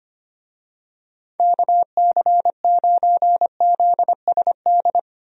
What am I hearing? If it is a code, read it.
Morse code, heard as a keyed tone, about 25 wpm, 700 Hz. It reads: KC9ZSD
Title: USA Call Signs - 6 Characters - Repeated 3x - 25wpm